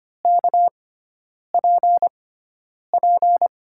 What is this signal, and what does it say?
Morse code, heard as a keyed tone, about 25 wpm, spaced as for 9 wpm, 700 Hz. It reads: KPP
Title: USA Call Signs - 5 Characters - 6x Character Spacing - 25wpm